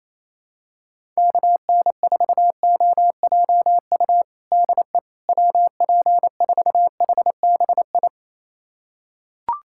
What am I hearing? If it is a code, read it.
Morse code, heard as a keyed tone, about 28 wpm, 700 Hz. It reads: KN4OJU DE WP4HBI
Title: USA Callsign DE Callsign 28wpm